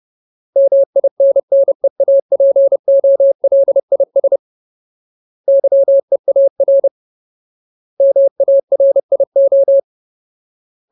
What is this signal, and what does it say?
Morse code, heard as a keyed tone, about 30 wpm, 550 Hz. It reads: MINNEAPOLIS YEAR MARIO
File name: Ultimate Word List / 30WPM Ultimate